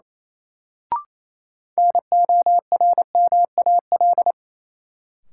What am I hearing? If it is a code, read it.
Morse code, heard as a keyed tone, about 28 wpm, 700 Hz. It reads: NORMAL
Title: Sentences from Top 1200 Words - New Word Drill Alt - 28wpm